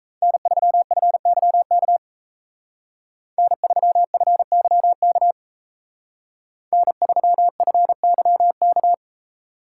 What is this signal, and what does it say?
Morse code, heard as a keyed tone, about 33 wpm, 700 Hz. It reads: N3FYK N3FYK N3FYK
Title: Speed Racing - USA 1x3 Call Signs 28wpm